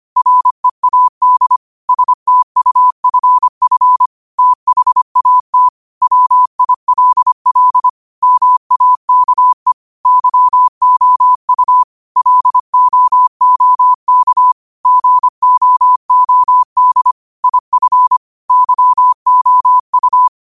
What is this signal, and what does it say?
Morse code, heard as a keyed tone, about 25 wpm, 1000 Hz. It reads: READ STUFF THAT WILL MAKE YOU LOOK GOOD IF YOU
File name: qotdfiles / 25wpm